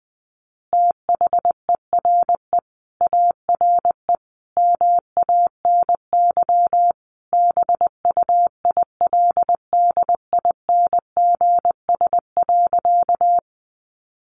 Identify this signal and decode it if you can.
Morse code, heard as a keyed tone, about 20 wpm, 700 Hz. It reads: THERE ARE MANY BUILDINGS.
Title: Easy Phrases 20wpm